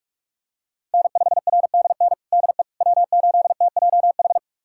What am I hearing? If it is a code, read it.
Morse code, heard as a keyed tone, about 46 wpm, 700 Hz. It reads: N5RDN DE W8TJH
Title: USA Callsign DE Callsign - 5 Characters - 45wpm